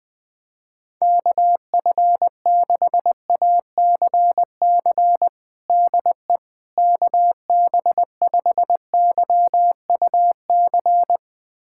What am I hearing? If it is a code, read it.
Morse code, heard as a keyed tone, about 20 wpm, 700 Hz. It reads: KF6ACC DE KB5YUC
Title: USA Callsign DE Callsign 20wpm